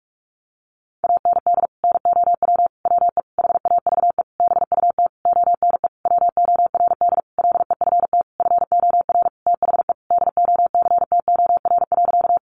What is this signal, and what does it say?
Morse code, heard as a keyed tone, about 44 wpm, 700 Hz. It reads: AND NOW WE HAVE BUT ONE WORD LEFT FOR THE DOCTOR.
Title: Sentences from Top 1500 Words 45wpm